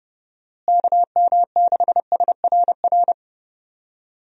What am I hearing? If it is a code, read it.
Morse code, heard as a keyed tone, about 32 wpm, 700 Hz. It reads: KM6SRR
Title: USA Call Signs - 6 Characters - Repeated 3x - 30wpm